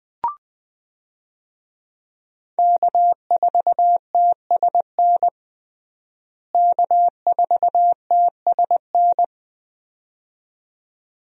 Morse code, heard as a keyed tone, about 20 wpm, 700 Hz. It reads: K4TSN K4TSN
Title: USA Call Signs - 5 Characters - Repeated 2x - 20wpm